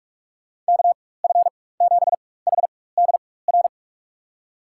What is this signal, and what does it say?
Morse code, heard as a keyed tone, about 42 wpm, spaced as for 20 wpm, 700 Hz. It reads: KF7HDR
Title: USA Call Signs - 6 Characters - 4x Character Spacing - 45wpm